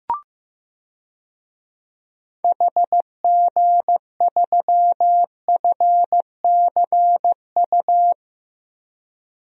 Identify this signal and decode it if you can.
Morse code, heard as a keyed tone, about 15 wpm, 700 Hz. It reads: HG3FCU
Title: DXCC-100 Call Signs - 6 Characters - Repeated 3x - 15wpm